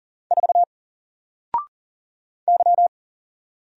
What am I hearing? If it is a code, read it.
Morse code, heard as a keyed tone, about 39 wpm, 700 Hz. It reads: 4 Y
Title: Single Letter-Number Alt 40wpm